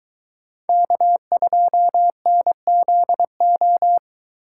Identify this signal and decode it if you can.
Morse code, heard as a keyed tone, about 23 wpm, 700 Hz. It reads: K2NZO